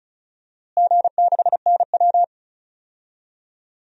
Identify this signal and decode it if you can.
Morse code, heard as a keyed tone, about 36 wpm, 700 Hz. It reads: G6NW
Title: DXCC-100 Call Signs - 4 Characters - Repeated 2x - 35wpm